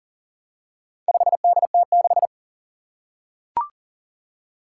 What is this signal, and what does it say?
Morse code, heard as a keyed tone, about 40 wpm, 700 Hz. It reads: 5DT6
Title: Four Challenging Letter-Number Combinations 40wpm